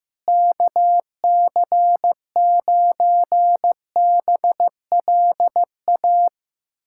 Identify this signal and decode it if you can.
Morse code, heard as a keyed tone, about 15 wpm, 700 Hz. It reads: KC9BLA
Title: USA Callsign DE Callsign 15wpm